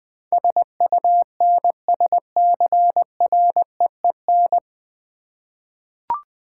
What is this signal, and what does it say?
Morse code, heard as a keyed tone, about 20 wpm, 700 Hz. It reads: SUNSCREEN
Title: Easy Words with Hints - 20wpm